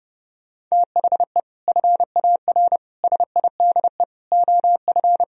Morse code, heard as a keyed tone, about 35 wpm, 700 Hz. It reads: THE FAR SIDE OF
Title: Speed-Racing - Sets of 5 Words 30wpm